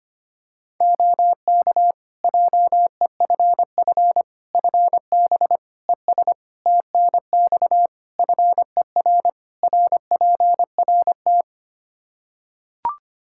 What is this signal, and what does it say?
Morse code, heard as a keyed tone, about 25 wpm, 700 Hz. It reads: OK JEFF FB ES TNX FER RPRT
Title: K1USN SST QSO Practice 25wpm